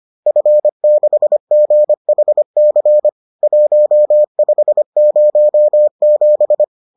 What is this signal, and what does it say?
Morse code, heard as a keyed tone, about 25 wpm, 600 Hz. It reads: F6GHC 1507
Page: bulletin UFT sous forme texte et en CW